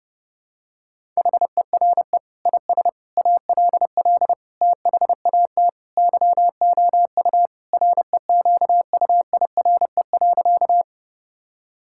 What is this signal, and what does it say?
Morse code, heard as a keyed tone, about 30 wpm, 700 Hz. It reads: HERE IS ALL THAT YOU REQUIRE.
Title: Sentences from Top 1000 Words - Review Part 6 - 30wpm